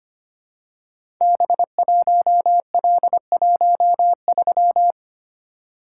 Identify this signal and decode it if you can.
Morse code, heard as a keyed tone, about 25 wpm, 700 Hz. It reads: B1L13
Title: Code Groups - 5 Letter:Numbers - 25wpm